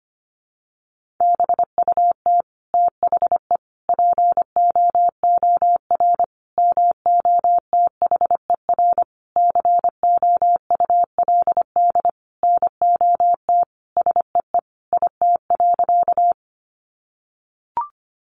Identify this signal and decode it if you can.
Morse code, heard as a keyed tone, about 25 wpm, 700 Hz. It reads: BUT THE POOR MOTHER COULD NOT SEE IT.
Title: Sentences from Top 1000 Words 25wpm